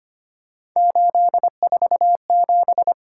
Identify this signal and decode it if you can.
Morse code, heard as a keyed tone, about 25 wpm, 700 Hz. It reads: 847
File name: Three Letter-Number Combinations - 25wpm